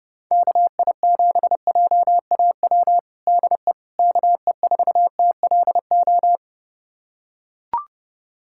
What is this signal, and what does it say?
Morse code, heard as a keyed tone, about 32 wpm, 700 Hz. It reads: KI7JAW DE KE4TLO
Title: USA Callsign DE Callsign 30wpm